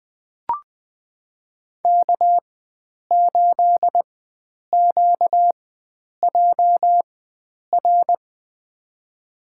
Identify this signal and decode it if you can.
Morse code, heard as a keyed tone, about 20 wpm, spaced as for 9 wpm, 700 Hz. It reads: K8QJR
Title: USA Call Signs - 5 Characters - 4x Character Spacing - 20wpm